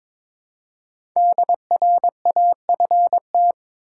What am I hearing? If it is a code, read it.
Morse code, heard as a keyed tone, about 22 wpm, 700 Hz. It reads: DRAFT
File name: 5 Letter Words - First 3 Letters Given - 22wpm